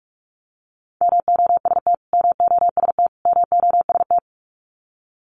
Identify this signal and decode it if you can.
Morse code, heard as a keyed tone, about 46 wpm, 700 Hz. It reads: MOST MOST MOST
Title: Top 500 Words Repeated 45wpm